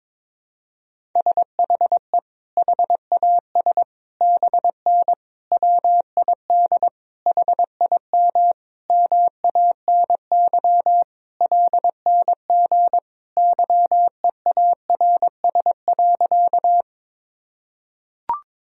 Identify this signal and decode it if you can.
Morse code, heard as a keyed tone, about 22 wpm, 700 Hz. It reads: SHE HAS BN WID HIM MANY LNG YEARS.
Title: Sentences from Top 200 Words and Abbreviations 22wpm